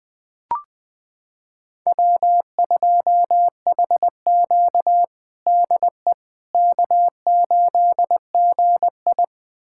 Morse code, heard as a keyed tone, about 20 wpm, 700 Hz. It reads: W2HQ DE K8GI
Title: USA Callsign DE Callsign - 4 Characters - 20wpm